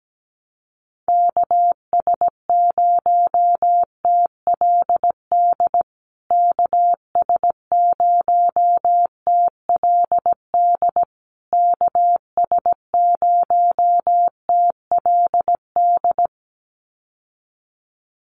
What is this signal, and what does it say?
Morse code, heard as a keyed tone, about 17 wpm, 700 Hz. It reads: KS0TLD KS0TLD KS0TLD
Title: USA Call Signs Repeated 3x 17wpm